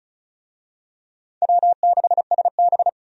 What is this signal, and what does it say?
Morse code, heard as a keyed tone, about 35 wpm, 700 Hz. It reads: W6SB
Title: USA Call Signs - 4 Characters - 35wpm